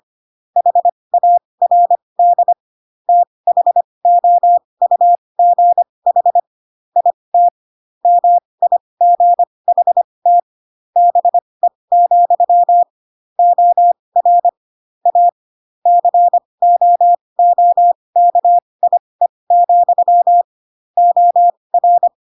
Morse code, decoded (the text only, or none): HARD THOUGH IT MIGHT BE, OR A COOKIE, OR